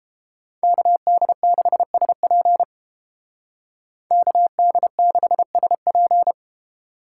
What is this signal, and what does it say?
Morse code, heard as a keyed tone, about 30 wpm, 700 Hz. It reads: KD6SP KD6SP